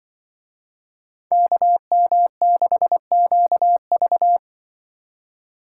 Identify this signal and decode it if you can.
Morse code, heard as a keyed tone, about 25 wpm, 700 Hz. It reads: KM6QV
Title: Speed Racing - USA 2x2 Call Signs 20wpm